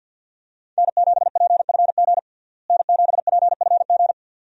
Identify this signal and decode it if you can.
Morse code, heard as a keyed tone, about 47 wpm, 700 Hz. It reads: N7PFG N7PFG
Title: USA Call Signs - 5 Characters - Repeated 2x - 50wpm